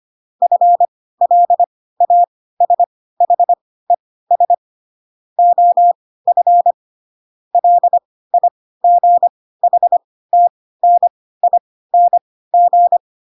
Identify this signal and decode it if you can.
Morse code, heard as a keyed tone, about 25 wpm, spaced as for 16 wpm, 700 Hz. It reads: FLASHES OF LIGHTNING
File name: Pinocchio / Pinocchio ch6 10 25 ik1hge